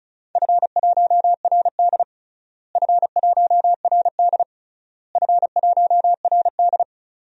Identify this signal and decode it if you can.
Morse code, heard as a keyed tone, about 34 wpm, 700 Hz. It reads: F1RD F1RD F1RD
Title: DXCC-100 Call Signs - 4 Characters - Repeated 3x - 35wpm